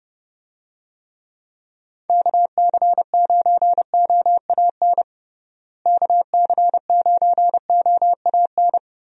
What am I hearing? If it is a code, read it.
Morse code, heard as a keyed tone, about 30 wpm, 700 Hz. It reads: KC9OAN KC9OAN
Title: USA Call Signs - 6 Characters - Repeated 3x - 30wpm